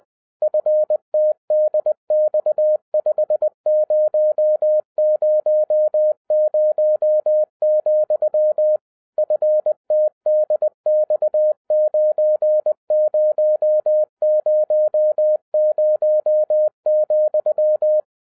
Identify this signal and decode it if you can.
Morse code, heard as a keyed tone, about 20 wpm, 610 Hz. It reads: FTDX5000, FTDX9000,
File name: 2011-07-mp3 / n58 BACK FROM A BIG ACCIDENT 20wpm